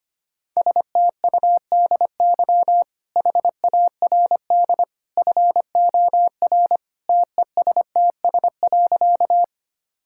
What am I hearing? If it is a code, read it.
Morse code, heard as a keyed tone, about 25 wpm, 690 Hz. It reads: STUDY HARD FOR TESTS.
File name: Easy Sentences with Hints - 25wpm